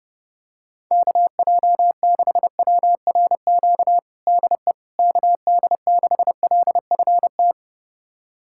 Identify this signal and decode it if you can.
Morse code, heard as a keyed tone, about 32 wpm, 700 Hz. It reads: KJ6WRQ DE KD6LFT
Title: USA Callsign DE Callsign - 6 Characters - 30wpm